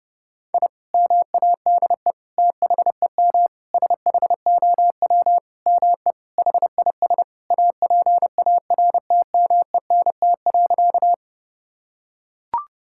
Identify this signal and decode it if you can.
Morse code, heard as a keyed tone, about 30 wpm, 700 Hz. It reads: I MADE THEM SHOW ME HIS APARTMENT.
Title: Sentences from Top 1000 Words - Review Part 6 - 30wpm